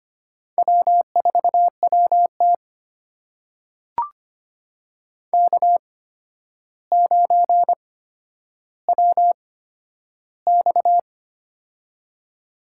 Morse code, heard as a keyed tone, about 25 wpm, spaced as for 11 wpm, 700 Hz. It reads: W4WT K9WX
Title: USA Call Signs - 4 Characters - 8x Character Spacing - 25wpm